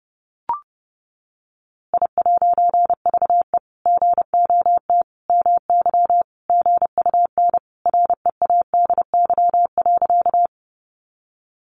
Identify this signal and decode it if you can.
Morse code, heard as a keyed tone, about 30 wpm, 700 Hz. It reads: I'VE GOT MY GUN READY.
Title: A Sound Of Thunder 30wpm